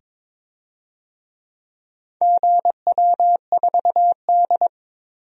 Morse code, heard as a keyed tone, about 22 wpm, 700 Hz. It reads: GW4D
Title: DXCC-100 Call Signs - 4 Characters - 22wpm